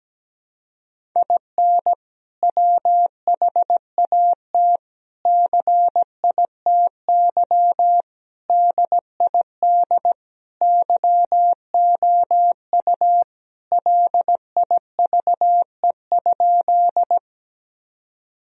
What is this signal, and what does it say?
Morse code, heard as a keyed tone, about 17 wpm, 700 Hz. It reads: IN WHAT CITY DID YOU LIVE?
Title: Sentences from Top 1000 Words - Review Part 1 - 17wpm